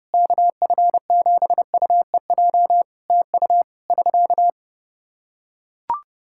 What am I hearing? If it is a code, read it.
Morse code, heard as a keyed tone, about 30 wpm, 700 Hz. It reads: KF7UEJ TU <SK>
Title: K1USN SST QSO Practice 30wpm